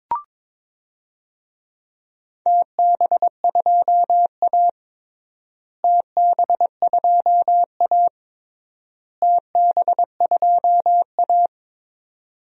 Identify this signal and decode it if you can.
Morse code, heard as a keyed tone, about 22 wpm, 700 Hz. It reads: TB2A TB2A TB2A
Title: DXCC-100 Call Signs - 4 Characters - Repeated 3x - 22wpm